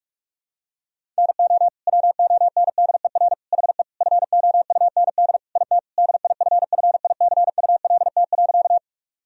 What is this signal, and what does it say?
Morse code, heard as a keyed tone, about 46 wpm, 700 Hz. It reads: NO WONDER HE FOUND IT DIFFICULT.
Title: Sentences from Top 1000 Words - Review Part 3 - 45wpm